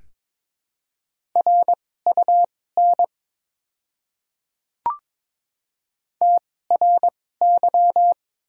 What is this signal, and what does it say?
Morse code, heard as a keyed tone, about 22 wpm, spaced as for 16 wpm, 700 Hz. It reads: RUN TRY
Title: 2x Character Spacing - Top 300 Words - 22wpm